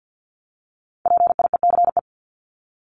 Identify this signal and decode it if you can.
Morse code, heard as a keyed tone, about 50 wpm, 700 Hz. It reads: PIECE
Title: Top 500 Words 50wpm